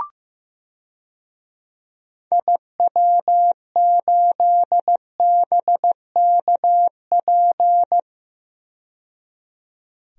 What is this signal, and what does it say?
Morse code, heard as a keyed tone, about 15 wpm, 700 Hz. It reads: IW8BKP
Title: DXCC-100 Call Signs - 6 Characters - 15wpm